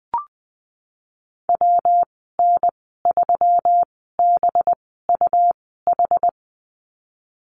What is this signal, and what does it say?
Morse code, heard as a keyed tone, about 20 wpm, spaced as for 15 wpm, 700 Hz. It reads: WN3BUH
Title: 193 Lesson 31g - Letter v - Call Signs 20wpm